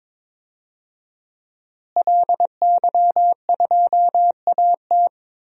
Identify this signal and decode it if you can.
Morse code, heard as a keyed tone, about 22 wpm, 700 Hz. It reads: LY2AT
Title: DXCC-100 Call Signs - 5 Characters - Repeated 2x - 22wpm